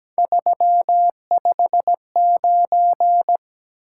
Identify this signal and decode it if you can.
Morse code, heard as a keyed tone, about 17 wpm, 700 Hz. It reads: 359
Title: Top 180 QSO Elements 17wpm